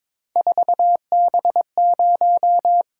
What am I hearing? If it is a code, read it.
Morse code, heard as a keyed tone, about 26 wpm, 700 Hz. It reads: 4B0
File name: Speed-Racing - Three Letter-Number Combinations - 22wpm